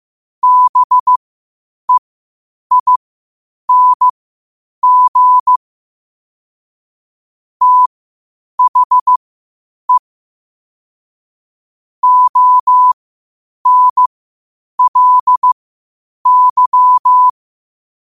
Morse code, heard as a keyed tone, about 15 wpm, spaced as for 8 wpm, 1000 Hz. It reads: BEING THE ONLY
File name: qotdfiles / 8wpm